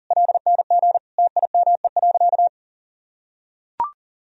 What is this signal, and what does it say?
Morse code, heard as a keyed tone, about 40 wpm, 700 Hz. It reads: LNG TIME.
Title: Sentences from Top 400 Words and Abbreviations 40wpm